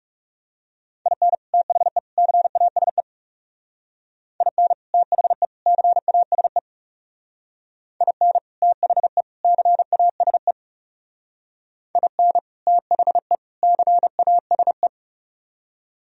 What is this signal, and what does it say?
Morse code, heard as a keyed tone, about 40 wpm, 700 Hz. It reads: IN THE CASE IN THE CASE IN THE CASE IN THE CASE